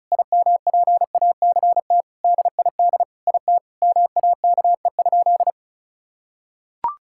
Morse code, heard as a keyed tone, about 36 wpm, 700 Hz. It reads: IMPACT DID IT MAKE?
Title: Sentences from Top 1000 Words - Review Part 6 - 35wpm